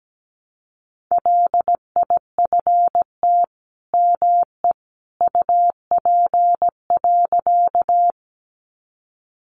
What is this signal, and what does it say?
Morse code, heard as a keyed tone, about 17 wpm, 700 Hz. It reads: LIFT ME UP.
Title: Sentences from Top 1000 Words 17wpm